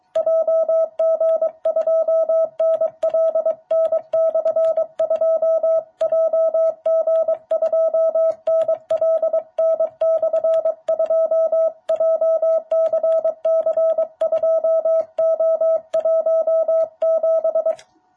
Morse code, heard as a keyed tone, about 23 wpm, 660 Hz. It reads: JG2NLN/2 JG2NLN/2 JCC2O17
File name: jh8cbh / 2021cw